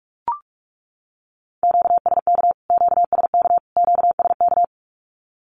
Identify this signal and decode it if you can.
Morse code, heard as a keyed tone, about 44 wpm, 700 Hz. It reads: QSK QSK QSK
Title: Q-Codes Repeated 3x 45wpm